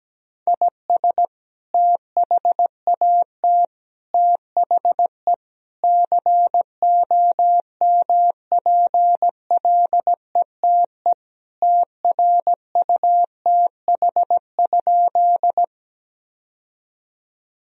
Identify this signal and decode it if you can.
Morse code, heard as a keyed tone, about 17 wpm, 700 Hz. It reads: IS THAT THE COMPLETE TRUTH?
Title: Sentences from Top 1000 Words - Review Part 2 - 17wpm